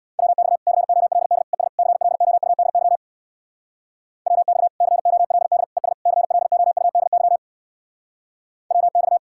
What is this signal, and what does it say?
Morse code, heard as a keyed tone, about 61 wpm, 700 Hz. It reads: WX COLD ES CLOUDY WX COLD ES CLOUDY WX
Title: Speed-Racing Common QSO Phrases - 50wpm